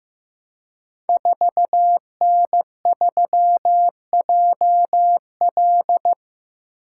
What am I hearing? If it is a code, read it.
Morse code, heard as a keyed tone, about 15 wpm, 700 Hz. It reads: 4N3JL
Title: Code Groups - 5 Letter:Numbers - 15wpm